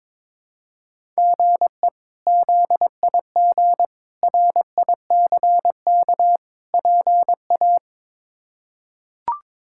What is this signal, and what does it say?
Morse code, heard as a keyed tone, about 22 wpm, 700 Hz. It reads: GE ZIG RICK PA
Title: K1USN SST QSO Practice - 22wpm